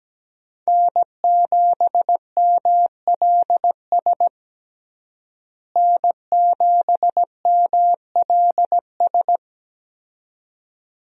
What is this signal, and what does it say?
Morse code, heard as a keyed tone, about 17 wpm, 700 Hz. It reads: N7MLS N7MLS
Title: USA Call Signs - 5 Characters - Repeated 2x - 17wpm